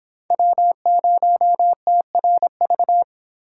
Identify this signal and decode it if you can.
Morse code, heard as a keyed tone, about 25 wpm, 700 Hz. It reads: W0TRV